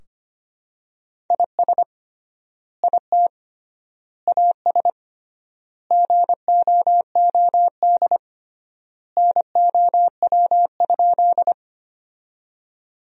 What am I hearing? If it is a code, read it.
Morse code, heard as a keyed tone, about 25 wpm, 700 Hz. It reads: IS IT AS GOOD NOW?
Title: 3x Word-Spacing - Sentences from Top 200 Words - 25wpm